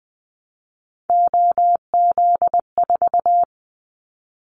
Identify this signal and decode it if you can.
Morse code, heard as a keyed tone, about 20 wpm, 700 Hz. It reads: OZ4
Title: Callsign Prefixes for Top 100 DXCC - 20wpm